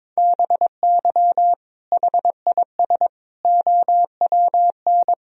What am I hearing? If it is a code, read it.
Morse code, heard as a keyed tone, about 22 wpm, 700 Hz. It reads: BY HIS OWN
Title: Sentences from Top 1000 Words - Review Part 1 - 22wpm